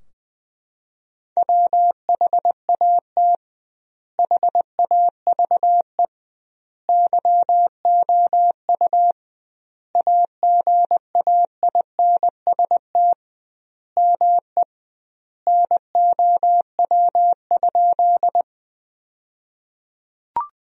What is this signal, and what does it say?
Morse code, heard as a keyed tone, about 20 wpm, 700 Hz. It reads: WHAT HAVE YOU AGAINST ME NOW?
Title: 2x Word-Spacing - Sentences from Top 400 Words - 20wpm